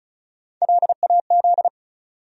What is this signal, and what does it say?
Morse code, heard as a keyed tone, about 36 wpm, 700 Hz. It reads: LAZ
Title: USA Call Sign Postfixes - 3 Letters 35wpm